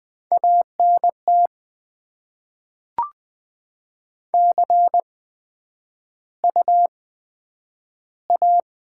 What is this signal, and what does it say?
Morse code, heard as a keyed tone, about 20 wpm, spaced as for 11 wpm, 700 Hz. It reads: ANT CU A
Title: Top 20 QSO Elements - 8x Character Spacing - 20wpm